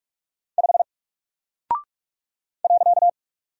Morse code, heard as a keyed tone, about 44 wpm, 700 Hz. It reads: 5 .
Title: Single Character (Letter:Number:Punctuation:Prosign) Alt 45wpm